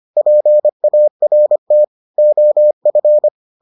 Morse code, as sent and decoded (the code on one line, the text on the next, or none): .--. .- .-. - / --- ..-.
PART OF